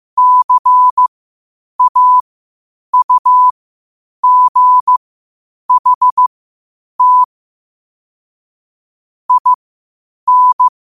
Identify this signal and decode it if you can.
Morse code, heard as a keyed tone, about 15 wpm, spaced as for 8 wpm, 1000 Hz. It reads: CAUGHT IN
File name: qotdfiles / 8wpm